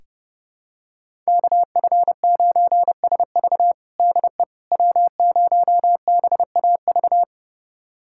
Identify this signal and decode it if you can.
Morse code, heard as a keyed tone, about 30 wpm, 700 Hz. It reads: KF9SV DE W0BAV
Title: USA Callsign DE Callsign - 5 Characters - 30wpm